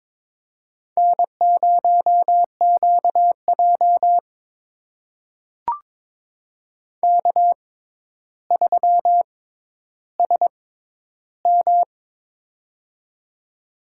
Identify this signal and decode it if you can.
Morse code, heard as a keyed tone, about 22 wpm, spaced as for 11 wpm, 700 Hz. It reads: N0QJ K3SM
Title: USA Call Signs - 4 Characters - 6x Character Spacing - 22wpm